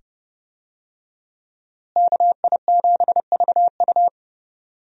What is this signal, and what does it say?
Morse code, heard as a keyed tone, about 30 wpm, 700 Hz. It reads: KI7VU
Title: USA Call Signs - 5 Characters - Repeated 2x - 30wpm